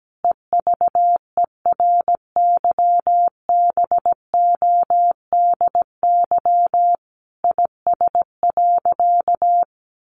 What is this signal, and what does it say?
Morse code, heard as a keyed tone, about 17 wpm, 700 Hz. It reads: EVERYBODY IS.
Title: Sentences from Top 1400 Words 17wpm